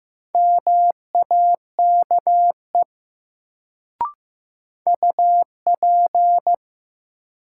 Morse code, heard as a keyed tone, about 15 wpm, 700 Hz. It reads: MAKE UP
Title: Sentences from Top 100 Words - New Word Drill - 15wpm